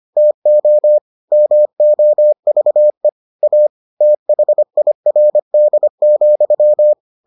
Morse code, heard as a keyed tone, about 25 wpm, 600 Hz. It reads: TO MOVE A THIRD,